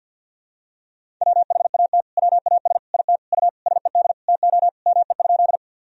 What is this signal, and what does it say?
Morse code, heard as a keyed tone, about 46 wpm, 700 Hz. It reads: WHAT WAS IT U SED TO ME?
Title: Sentences from Top 200 Words and Abbreviations 50wpm